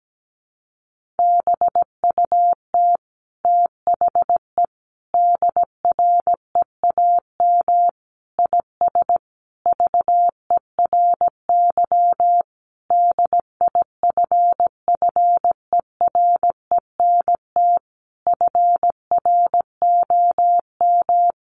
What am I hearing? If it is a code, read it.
Morse code, heard as a keyed tone, about 17 wpm, 700 Hz. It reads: BUT THE DREAM IS VERY DIFFERENT FROM
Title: Sentences from Top 1500 Words 17wpm